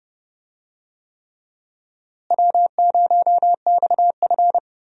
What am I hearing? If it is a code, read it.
Morse code, heard as a keyed tone, about 30 wpm, 700 Hz. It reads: W0XF
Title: USA Call Signs - 4 Characters - Repeated 3x - 30wpm